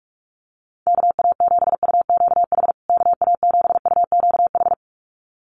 Morse code, heard as a keyed tone, about 44 wpm, 700 Hz. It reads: KA7UQH KA7UQH
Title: USA Call Signs Repeated 2x 45wpm